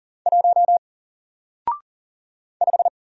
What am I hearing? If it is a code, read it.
Morse code, heard as a keyed tone, about 40 wpm, 700 Hz. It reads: 1 5
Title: Single Letter-Number Alt 40wpm